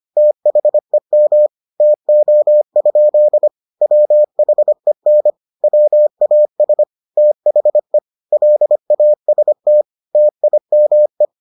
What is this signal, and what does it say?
Morse code, heard as a keyed tone, about 25 wpm, 600 Hz. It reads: THEM TO? WHEN WAS THE LAST TIME